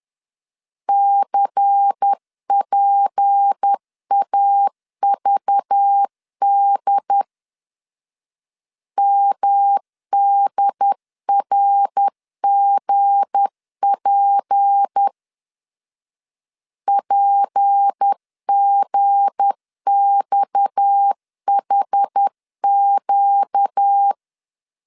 Morse code, heard as a keyed tone, about 10 wpm, 800 Hz. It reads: CPAVD MDRGP PGXHQ
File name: LEZIONI / 7 GF9Z / 7 G 10wpm 2501